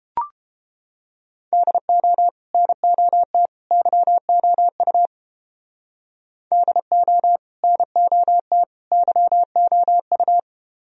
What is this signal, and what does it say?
Morse code, heard as a keyed tone, about 31 wpm, 700 Hz. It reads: DO NOT YOU DO NOT YOU